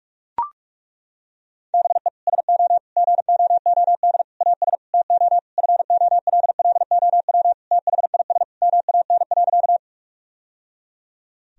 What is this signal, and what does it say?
Morse code, heard as a keyed tone, about 41 wpm, 700 Hz. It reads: BE SO GOOD AS TO FOLLOW THIS MAN.
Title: Sentences from Top 1000 Words - Review Part 1 - 45wpm